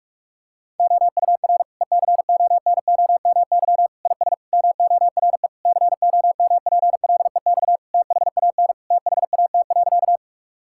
Morse code, heard as a keyed tone, about 45 wpm, 700 Hz. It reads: OUR ECONOMY IS MORE COMPLEX THAN THAT.
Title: Sentences from Top 1000 Words - Review Part 6 - 45wpm